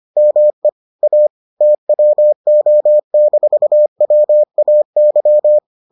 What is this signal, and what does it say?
Morse code, heard as a keyed tone, about 25 wpm, 600 Hz. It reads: ME A TWO-WAY